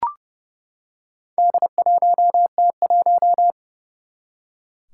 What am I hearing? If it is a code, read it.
Morse code, heard as a keyed tone, about 30 wpm, 700 Hz. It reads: D1T1
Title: Four Challenging Letter-Number Combinations 30wpm